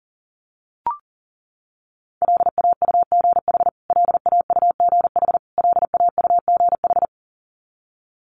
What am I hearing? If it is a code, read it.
Morse code, heard as a keyed tone, about 40 wpm, 700 Hz. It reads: LAUGH LAUGH LAUGH
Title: Top 500 Words Repeated 40wpm